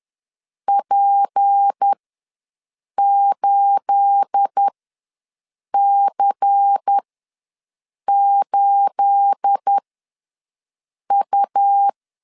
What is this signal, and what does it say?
Morse code, heard as a keyed tone, about 10 wpm, spaced as for 6 wpm, 800 Hz. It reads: P8C8U
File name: LEZIONI / 9 28 ws 10wpm 4124